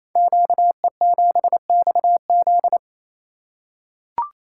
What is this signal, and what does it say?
Morse code, heard as a keyed tone, about 28 wpm, 700 Hz. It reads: QE7XZ
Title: Code Groups - 5 Letter:Numbers - 28wpm